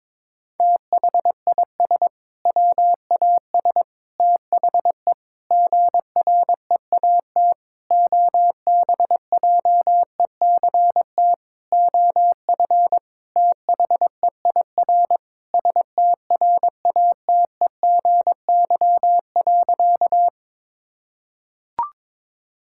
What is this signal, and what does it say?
Morse code, heard as a keyed tone, about 22 wpm, 700 Hz. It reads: THIS WAS THE GREAT OBJECT OF THEIR STRATEGY.
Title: Sentences from Top 1000 Words - Review Part 5 - 22wpm